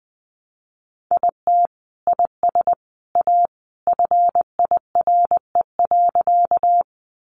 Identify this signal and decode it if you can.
Morse code, heard as a keyed tone, about 19 wpm, 700 Hz. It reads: IT IS A FIRE.
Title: Sentences from Top 400 Words 20wpm